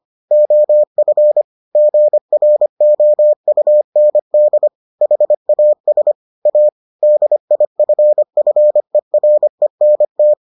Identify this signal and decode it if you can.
Morse code, heard as a keyed tone, about 25 wpm, 600 Hz. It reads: OF GROUND HAS A DIFFERENT